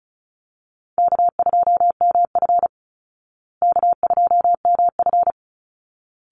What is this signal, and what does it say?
Morse code, heard as a keyed tone, about 34 wpm, 700 Hz. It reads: K2MF K2MF
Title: Easy Call Signs Repeated 2x - 35wpm